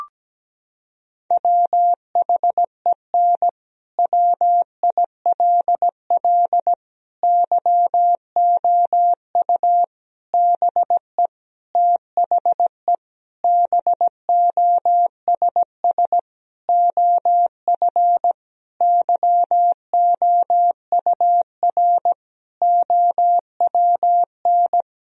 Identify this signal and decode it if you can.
Morse code, heard as a keyed tone, about 17 wpm, 700 Hz. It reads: WHEN WILL YOU BE THE BOSS OF YOUR OWN
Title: Sentences from Top 1000 Words - Review Part 4 - 17wpm